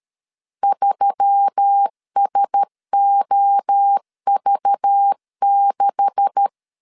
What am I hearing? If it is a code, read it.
Morse code, heard as a keyed tone, about 13 wpm, 800 Hz. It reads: 3SOV6
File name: LEZIONI / 9 2867 12wpm 2322